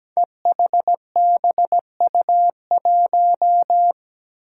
Morse code, heard as a keyed tone, about 17 wpm, 700 Hz. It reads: EHBU1
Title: Five Challenging Letter-Number Combinations 17wpm